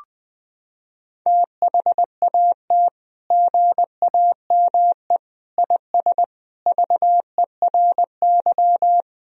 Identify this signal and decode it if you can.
Morse code, heard as a keyed tone, about 21 wpm, 700 Hz. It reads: THAT GAME IS VERY
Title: Sentences from Top 1000 Words - Review Part 3 - 20wpm